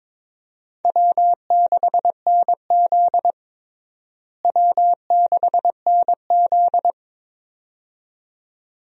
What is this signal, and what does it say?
Morse code, heard as a keyed tone, about 22 wpm, 700 Hz. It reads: W6NZ W6NZ
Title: USA Call Signs - 4 Characters - Repeated 2x - 22wpm